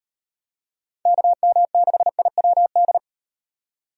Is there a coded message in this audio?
Yes: Morse code from 1.1 to 3.0 s, heard as a keyed tone, about 34 wpm, 700 Hz. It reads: KM6IWD